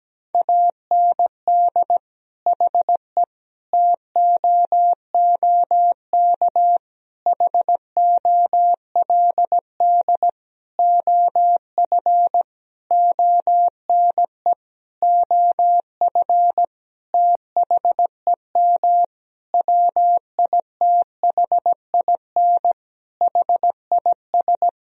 Morse code, heard as a keyed tone, about 17 wpm, 700 Hz. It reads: AND HE TOOK HOLD OF ONE OF THEM WITHIN HIS
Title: Sentences from Top 1000 Words - Review Part 4 - 17wpm